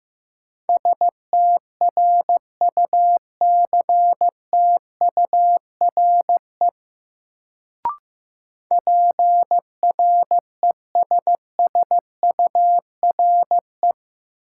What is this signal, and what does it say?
Morse code, heard as a keyed tone, about 15 wpm, 700 Hz. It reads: STRUCTURE PRESSURE
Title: Sentences from Top 1000 Words - Review Part 6 - New Word Drill - 15wpm